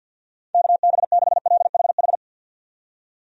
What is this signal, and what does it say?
Morse code, heard as a keyed tone, about 53 wpm, 700 Hz. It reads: KB6LHH
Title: USA Call Signs - 6 Characters - 50wpm